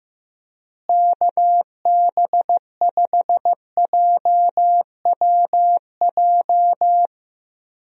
Morse code, heard as a keyed tone, about 15 wpm, 700 Hz. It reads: KB5JWJ
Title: USA Call Signs - 6 Characters - Repeated 2x - 15wpm